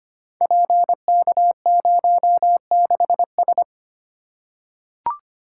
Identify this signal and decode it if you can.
Morse code, heard as a keyed tone, about 25 wpm, 700 Hz. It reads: PK06S
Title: Code Groups - 5 Letter:Numbers - 25wpm